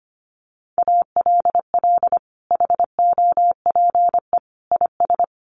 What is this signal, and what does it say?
Morse code, heard as a keyed tone, about 25 wpm, 700 Hz. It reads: ALL HOPE IS
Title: Sentences from Top 1400 Words 25wpm